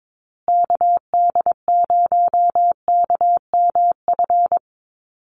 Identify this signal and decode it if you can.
Morse code, heard as a keyed tone, about 22 wpm, 700 Hz. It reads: KD0KMF
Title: USA Call Signs Repeated 2x 22wpm